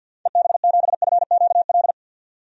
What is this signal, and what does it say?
Morse code, heard as a keyed tone, about 50 wpm, 700 Hz. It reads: EB7FQL
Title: DXCC-100 Call Signs - 6 Characters - 50wpm